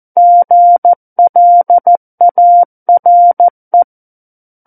Morse code, heard as a keyed tone, about 14 wpm, 700 Hz. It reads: GLARE
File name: Book One / 05-Chapter 5 - 14 WPM